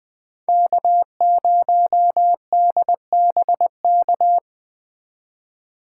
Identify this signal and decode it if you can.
Morse code, heard as a keyed tone, about 20 wpm, 700 Hz. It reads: K0DBK